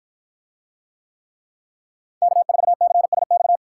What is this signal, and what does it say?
Morse code, heard as a keyed tone, about 49 wpm, 700 Hz. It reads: K4CSX